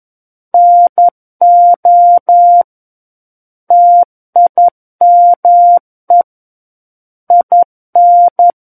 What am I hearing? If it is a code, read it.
Morse code, heard as a keyed tone, about 11 wpm, 700 Hz. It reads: NO TIME IN
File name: 11 WPM